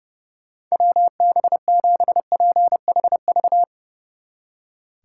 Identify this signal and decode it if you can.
Morse code, heard as a keyed tone, about 40 wpm, 700 Hz. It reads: WB7PHV